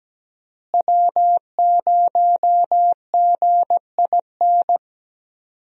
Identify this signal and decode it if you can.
Morse code, heard as a keyed tone, about 17 wpm, 700 Hz. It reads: W0GIN
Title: USA Call Signs - 5 Characters - Repeated 2x - 17wpm